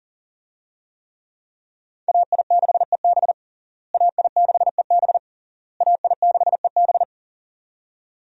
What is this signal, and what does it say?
Morse code, heard as a keyed tone, about 40 wpm, 700 Hz. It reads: AI6EB AI6EB AI6EB
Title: USA Call Signs - 5 Characters - Repeated 3x - 40wpm